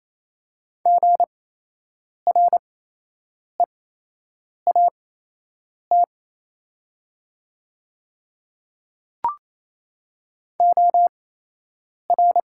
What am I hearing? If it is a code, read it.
Morse code, heard as a keyed tone, about 27 wpm, spaced as for 8 wpm, 700 Hz. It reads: GREAT OR
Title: 8x Character Spacing - Top 200 Words - 28wpm